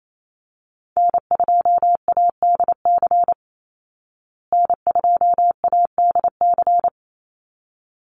Morse code, heard as a keyed tone, about 28 wpm, 700 Hz. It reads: N2ADC N2ADC
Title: Call Signs Repeated 3x - 28wpm